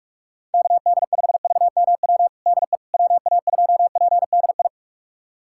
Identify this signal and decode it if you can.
Morse code, heard as a keyed tone, about 42 wpm, 700 Hz. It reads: KD5VMW DE WA2PDI
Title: USA Callsign DE Callsign - 6 Characters - 45wpm